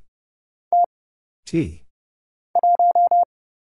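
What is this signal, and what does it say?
Morse code, heard as a keyed tone, about 29 wpm, 700 Hz. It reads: T 1
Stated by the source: Morse Code Ninja single-character practice (letters, numbers, punctuation, prosigns) at 30 wpm, by Kurt Zoglmann (AD0WE).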